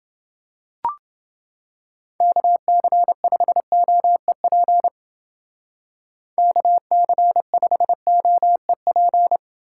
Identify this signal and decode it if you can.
Morse code, heard as a keyed tone, about 29 wpm, 700 Hz. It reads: KC5OEP KC5OEP